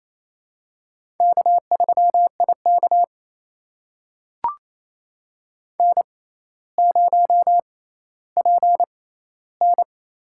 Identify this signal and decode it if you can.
Morse code, heard as a keyed tone, about 28 wpm, spaced as for 14 wpm, 700 Hz. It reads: K3IK N0PN
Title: USA Call Signs - 4 Characters - 6x Character Spacing - 28wpm